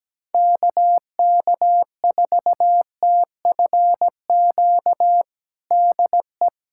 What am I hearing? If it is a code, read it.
Morse code, heard as a keyed tone, about 17 wpm, 700 Hz. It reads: KK4TFQ DE
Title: USA Callsign DE Callsign - 6 Characters - 17wpm